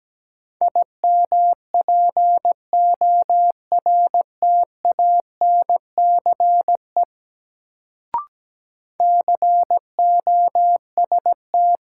Morse code, heard as a keyed tone, about 17 wpm, 700 Hz. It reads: IMPORTANCE COST
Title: Sets of 3 Words - New Word Drill - 17wpm